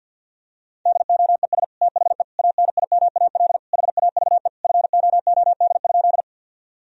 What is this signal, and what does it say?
Morse code, heard as a keyed tone, about 48 wpm, 700 Hz. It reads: DOES THE ANIMAL HAVE FOOD?
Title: Sentences from Top 1000 Words - Review Part 1 - 50wpm